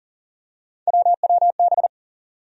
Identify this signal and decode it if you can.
Morse code, heard as a keyed tone, about 40 wpm, 700 Hz. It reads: WWB
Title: USA Call Sign Postfixes - 3 Letters 40wpm